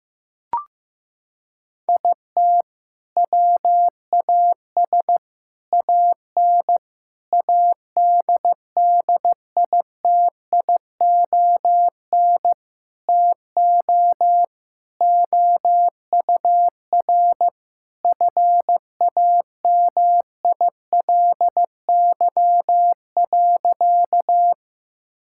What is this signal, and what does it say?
Morse code, heard as a keyed tone, about 15 wpm, 700 Hz. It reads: IT WAS AN ADDITION TO OUR FAMILY.
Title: Sentences from Top 1000 Words - Review Part 6 - 15wpm